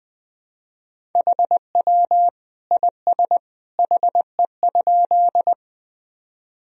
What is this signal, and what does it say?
Morse code, heard as a keyed tone, about 21 wpm, 700 Hz. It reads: HW IS HE?
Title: Sentences from Top 100 Words and Abbreviations 20wpm